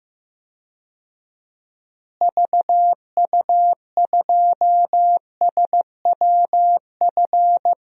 Morse code, heard as a keyed tone, about 15 wpm, 700 Hz. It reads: VU2SWF
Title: DXCC-100 Call Signs - 6 Characters - Repeated 2x - 15wpm